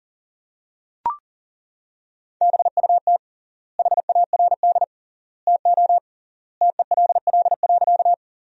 Morse code, heard as a keyed tone, about 37 wpm, 700 Hz. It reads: BUT HARD TO TELL.
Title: 3x Word-Spacing - Sentences from Top 300 Words - 40wpm